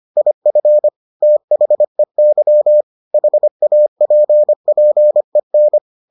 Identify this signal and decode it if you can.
Morse code, heard as a keyed tone, about 25 wpm, 600 Hz. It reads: IF THEY HAPPEN